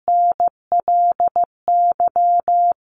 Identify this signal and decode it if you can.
Morse code, heard as a keyed tone, about 15 wpm, 700 Hz. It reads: NLY
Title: Sentences from Top 900 Words 15wpm